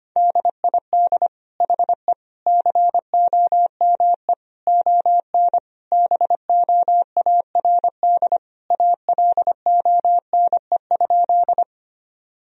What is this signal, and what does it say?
Morse code, heard as a keyed tone, about 25 wpm, 700 Hz. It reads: DID HE COME ON BOARD ALONE?
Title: Sentences from Top 1000 Words - Review Part 4 - 25wpm